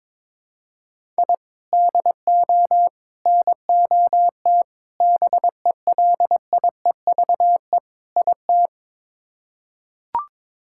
Morse code, heard as a keyed tone, about 22 wpm, 700 Hz. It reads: I DO NOT BELIEVE IT
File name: practice / 22wpm / Build A Phrase - 5-Word Phrases - 22wpm